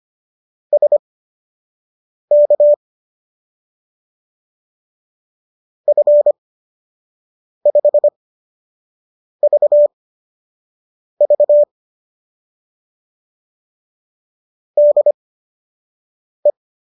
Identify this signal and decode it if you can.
Morse code, heard as a keyed tone, about 25 wpm, spaced as for 6 wpm, 590 Hz. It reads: SK F5VV DE